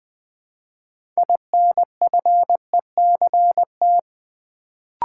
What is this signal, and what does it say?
Morse code, heard as a keyed tone, about 20 wpm, 700 Hz. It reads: INFECT
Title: Build A Word - 3 Letters At A Time - 6-Letter Words - 20wpm